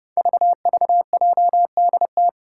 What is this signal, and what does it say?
Morse code, heard as a keyed tone, about 30 wpm, 700 Hz. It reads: VVJDT
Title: Five Challenging Letter-Number Combinations 30wpm